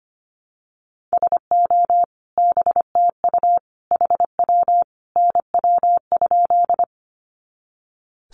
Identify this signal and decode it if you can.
Morse code, heard as a keyed tone, about 25 wpm, 700 Hz. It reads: SO BTU HW NW?
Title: Common QSO Phrases 25wpm